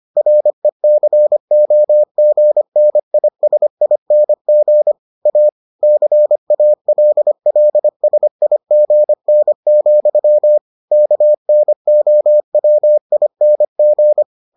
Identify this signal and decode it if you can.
Morse code, heard as a keyed tone, about 25 wpm, 600 Hz. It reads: RECOGNISING A CALLSIGN, KNOWING